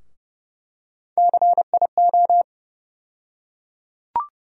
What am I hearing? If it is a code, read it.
Morse code, heard as a keyed tone, about 31 wpm, 700 Hz. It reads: CIO
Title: USA Call Sign Postfixes - 3 Letters 30wpm